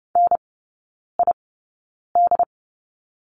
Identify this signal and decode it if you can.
Morse code, heard as a keyed tone, about 30 wpm, spaced as for 9 wpm, 700 Hz. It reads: NID
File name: course / 30:10wpm / 087 Lesson 15e - Number 5 - Three Characters 30wpm